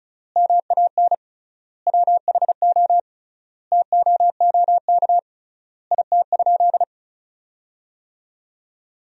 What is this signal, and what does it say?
Morse code, heard as a keyed tone, about 36 wpm, 700 Hz. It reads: MAN WHO TOOK IT?
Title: 3x Word-Spacing - Sentences from Top 300 Words - 35wpm